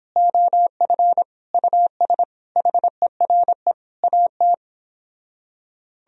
Morse code, heard as a keyed tone, about 29 wpm, 700 Hz. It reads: OF US HERE AT